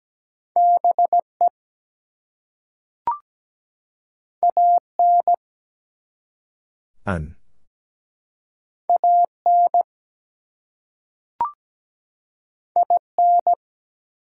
Morse code, heard as a keyed tone, about 17 wpm, 700 Hz. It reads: BE AN AN IN
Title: Two Letter Words - 17wpm